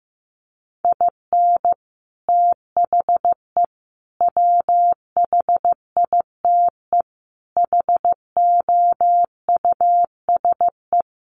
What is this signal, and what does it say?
Morse code, heard as a keyed tone, about 15 wpm, 700 Hz. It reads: IN THE WHITE HOUSE
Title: Sets of 4 Words 15wpm